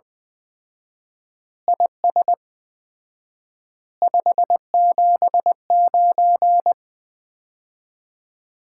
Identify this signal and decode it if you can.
Morse code, heard as a keyed tone, about 20 wpm, 700 Hz. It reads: IS 579
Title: Common QSO Phrases - 4x Word-Spacing - 20wpm